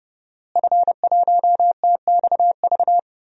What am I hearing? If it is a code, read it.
Morse code, heard as a keyed tone, about 30 wpm, 700 Hz. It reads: F1TXV
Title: DXCC-100 Call Signs - 5 Characters - Repeated 2x - 30wpm